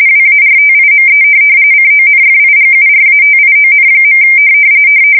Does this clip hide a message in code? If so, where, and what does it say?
Yes: RTTY from 0.1 to 4.2 s, about 45.45 baud. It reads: WARTIME CONFIGURATION.